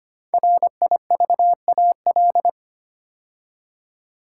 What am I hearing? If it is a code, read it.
Morse code, heard as a keyed tone, about 25 wpm, 700 Hz. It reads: RIVAL
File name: 5 Letter Words - First 3 Letters Given - 25wpm